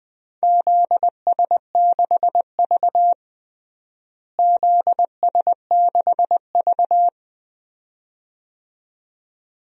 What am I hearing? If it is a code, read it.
Morse code, heard as a keyed tone, about 21 wpm, 700 Hz. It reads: ZS6V ZS6V
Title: DXCC-100 Call Signs - 4 Characters - Repeated 2x - 20wpm